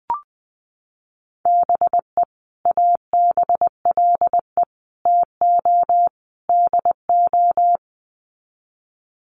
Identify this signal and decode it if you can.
Morse code, heard as a keyed tone, about 20 wpm, 700 Hz. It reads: BE ABLE TO DO
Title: Sets of 4 Words 20wpm